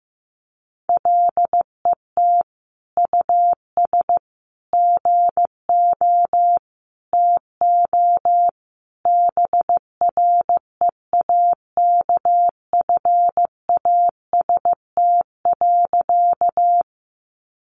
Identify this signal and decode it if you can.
Morse code, heard as a keyed tone, about 15 wpm, 700 Hz. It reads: LET US GO TO BREAKFAST.
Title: Sentences from Top 1500 Words 15wpm